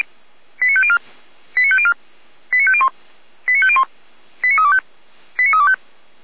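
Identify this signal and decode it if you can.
Five-tone selective call (ZVEI), heard as quick runs of five tones, tones of about 70 ms. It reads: 89504 89504 89501 89501 89325 89325
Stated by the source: ZVEI five-tone selective call 89333 received on SDR.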